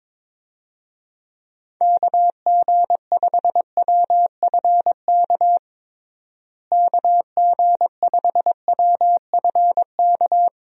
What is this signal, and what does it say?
Morse code, heard as a keyed tone, about 21 wpm, 700 Hz. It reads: KG5WFK KG5WFK
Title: USA Call Signs - 6 Characters - Repeated 2x - 22wpm